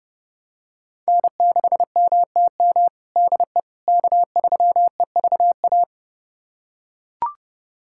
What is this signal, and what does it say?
Morse code, heard as a keyed tone, about 30 wpm, 700 Hz. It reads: N6MTM DE K3EVA
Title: USA Callsign DE Callsign - 5 Characters - 30wpm